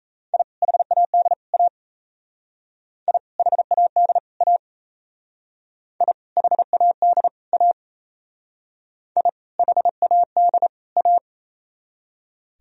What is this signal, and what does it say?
Morse code, heard as a keyed tone, about 35 wpm, 700 Hz. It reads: I HAD A I HAD A I HAD A I HAD A